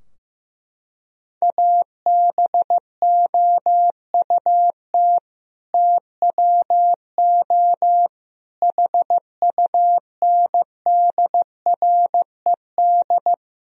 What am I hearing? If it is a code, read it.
Morse code, heard as a keyed tone, about 15 wpm, 700 Hz. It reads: ABOUT TWO HUNDRED
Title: Sentences from Top 1000 Words - Review Part 2 - 15wpm